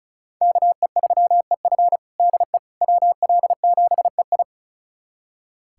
Morse code, heard as a keyed tone, about 36 wpm, 700 Hz. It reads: KE3EF DE WL7EI
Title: USA Callsign DE Callsign - 5 Characters - 35wpm